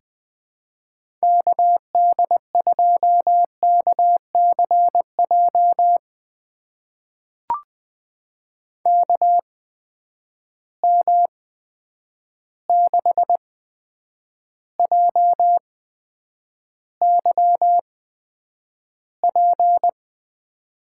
Morse code, heard as a keyed tone, about 21 wpm, spaced as for 9 wpm, 700 Hz. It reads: KD2KCJ KM6JYP
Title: USA Call Signs - 6 Characters - 8x Character Spacing - 20wpm